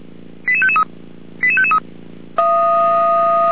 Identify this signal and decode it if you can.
Five-tone selective call (ZVEI), heard as quick runs of five tones, tones of about 70 ms. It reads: 88553 88553